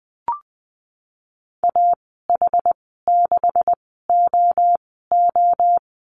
Morse code, heard as a keyed tone, about 20 wpm, spaced as for 15 wpm, 700 Hz. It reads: AH6OO
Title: 147 Lesson 24g - Letter y - Call Signs 20wpm